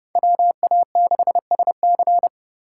Morse code, heard as a keyed tone, about 27 wpm, 700 Hz. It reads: WA6SC